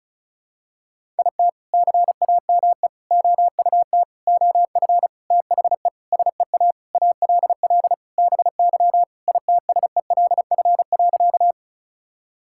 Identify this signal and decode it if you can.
Morse code, heard as a keyed tone, about 34 wpm, 700 Hz. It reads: IT CAME OUT OF THE SEA ALL BY ITSELF.
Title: Sentences from Top 1000 Words - Review Part 5 - 35wpm